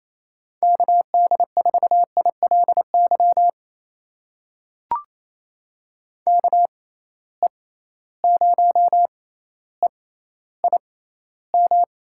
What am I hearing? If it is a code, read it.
Morse code, heard as a keyed tone, about 28 wpm, spaced as for 14 wpm, 700 Hz. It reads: KD4ILY KE0EIM
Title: USA Call Signs - 6 Characters - 6x Character Spacing - 28wpm